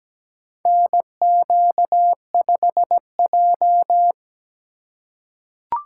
Morse code, heard as a keyed tone, about 17 wpm, 700 Hz. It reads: NQ5J
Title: USA Call Signs - 4x Character Spacing - 17wpm